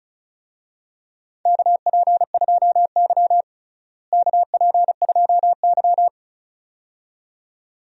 Morse code, heard as a keyed tone, about 36 wpm, 700 Hz. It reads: KP2Y KP2Y
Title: DXCC-100 Call Signs - 4 Characters - Repeated 2x - 35wpm